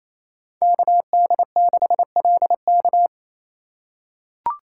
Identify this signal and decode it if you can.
Morse code, heard as a keyed tone, about 28 wpm, 690 Hz. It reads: KD6LK
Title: USA Call Signs - 2x Character Spacing - 28wpm